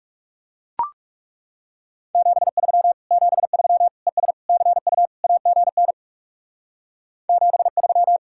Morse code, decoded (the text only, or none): 73 73 ES CU AGN 73